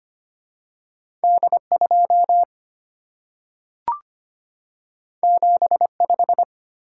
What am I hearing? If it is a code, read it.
Morse code, heard as a keyed tone, about 25 wpm, 700 Hz. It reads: D2 75
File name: Two Letter-Number Combinations - 25wpm